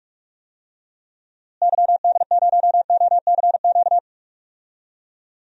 Morse code, heard as a keyed tone, about 46 wpm, 700 Hz. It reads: YD0OCQ
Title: DXCC-100 Call Signs - 6 Characters - Repeated 3x - 45wpm